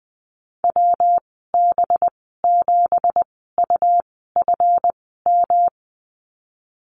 Morse code, heard as a keyed tone, about 20 wpm, spaced as for 14 wpm, 700 Hz. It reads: WB7UFM
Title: 231 Lesson 37g - Letter x - Call Signs 20wpm